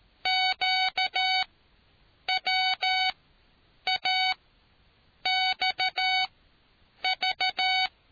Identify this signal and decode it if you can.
Morse code, heard as a keyed tone, about 13 wpm, spaced as for 7 wpm, 750 Hz. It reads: QWAXV